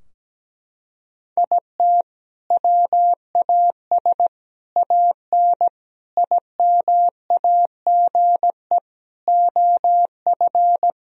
Morse code, heard as a keyed tone, about 17 wpm, 700 Hz. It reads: IT WAS AN IMAGE OF
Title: Sentences from Top 1000 Words - Review Part 4 - 17wpm